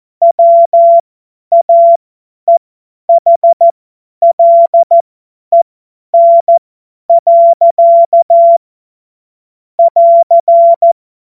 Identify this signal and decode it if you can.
Morse code, heard as a keyed tone, about 14 wpm, spaced as for 10 wpm, 680 Hz. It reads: WAEHLEN. <AR>
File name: Morsekurs / Witze / Witz 03 Juni 2025 50 F70